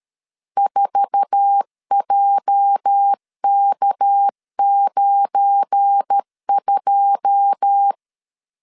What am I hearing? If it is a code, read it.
Morse code, heard as a keyed tone, about 13 wpm, 800 Hz. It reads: 4JK92
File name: CW / DET / 1 9 1500C S C 12wpm 3811